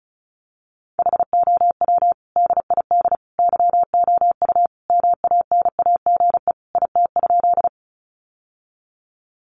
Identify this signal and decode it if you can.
Morse code, heard as a keyed tone, about 34 wpm, 700 Hz. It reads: HOW DID YOU MANAGE IT?
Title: Sentences from Top 1400 Words 35wpm